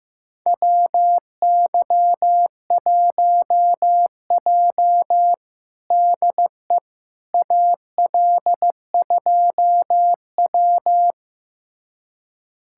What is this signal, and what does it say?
Morse code, heard as a keyed tone, about 15 wpm, 700 Hz. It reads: WY1J DE AL2W
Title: USA Callsign DE Callsign - 4 Characters - 15wpm